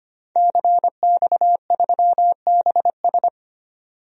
Morse code, heard as a keyed tone, about 25 wpm, 700 Hz. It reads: CX3BS
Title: DXCC-100 Call Signs - 5 Characters - 25wpm